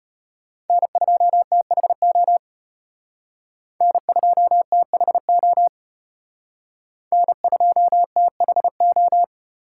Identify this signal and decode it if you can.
Morse code, heard as a keyed tone, about 32 wpm, 700 Hz. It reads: N2THO N2THO N2THO